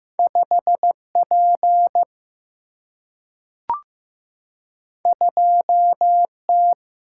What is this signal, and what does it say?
Morse code, heard as a keyed tone, about 15 wpm, 700 Hz. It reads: 5 P 2 T
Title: Single-Letter Single-Number 15wpm